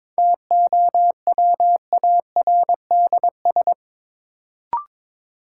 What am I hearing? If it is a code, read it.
Morse code, heard as a keyed tone, about 22 wpm, 700 Hz. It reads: TOWARDS
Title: Sentences from Top 1000 Words - Review Part 2 - New Word Drill - 22wpm